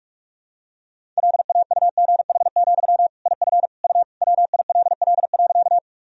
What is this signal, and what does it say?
Morse code, heard as a keyed tone, about 46 wpm, 690 Hz. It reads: LAUGH, IF U WILL.
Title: Sentences from Top 500 Words and Abbreviations 45wpm